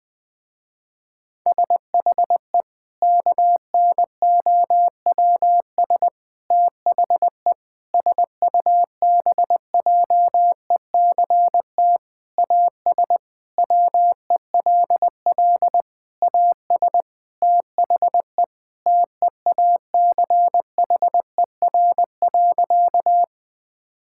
Morse code, heard as a keyed tone, about 20 wpm, 700 Hz. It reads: SHE KNOWS THE SUBJECT AS WELL AS THE TEACHER.
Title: Sentences with 4-Word Phrases - Spoken Beginning - 20wpm